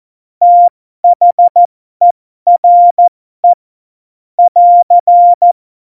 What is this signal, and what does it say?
Morse code, heard as a keyed tone, about 14 wpm, spaced as for 12 wpm, 700 Hz. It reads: THERE <AR>
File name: Morsekurs / Witze